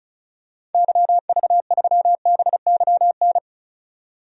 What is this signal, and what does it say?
Morse code, heard as a keyed tone, about 36 wpm, 700 Hz. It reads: YV3BYN
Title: DXCC-100 Call Signs - 6 Characters - 35wpm